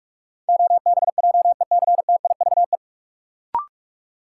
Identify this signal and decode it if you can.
Morse code, heard as a keyed tone, about 46 wpm, 700 Hz. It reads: OBJECTIVE
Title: Sentences from Top 1200 Words - New Word Drill Alt - 45wpm